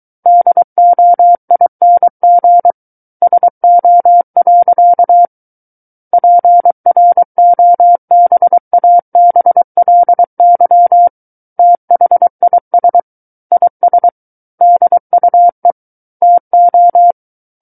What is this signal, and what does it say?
Morse code, heard as a keyed tone, about 23 wpm, 700 Hz. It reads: DOING SO. PROBABLY THIS IS DUE TO
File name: Book One / 14-Chapter 14 - 23 WPM